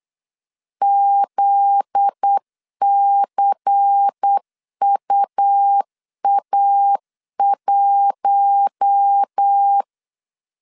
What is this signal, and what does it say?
Morse code, heard as a keyed tone, about 8 wpm, 800 Hz. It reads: ZCUA1
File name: LEZIONI / 7 GF9Z / 7 GF9Z 8wpm 3213